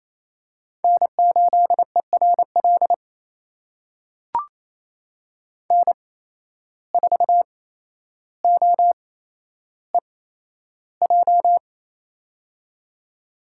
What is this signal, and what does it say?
Morse code, heard as a keyed tone, about 28 wpm, spaced as for 12 wpm, 700 Hz. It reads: N8ERL N4OEJ
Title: USA Call Signs - 5 Characters - 8x Character Spacing - 28wpm